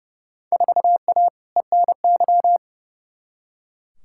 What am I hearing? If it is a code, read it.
Morse code, heard as a keyed tone, about 30 wpm, 700 Hz. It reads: 4A ENY
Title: Field Day Exchanges Encore - 30wpm